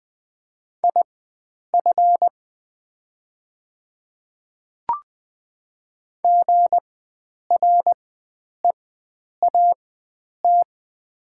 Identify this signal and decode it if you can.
Morse code, heard as a keyed tone, about 20 wpm, spaced as for 9 wpm, 700 Hz. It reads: IF GREAT
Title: 4x Character Spacing - Top 300 Words - 20wpm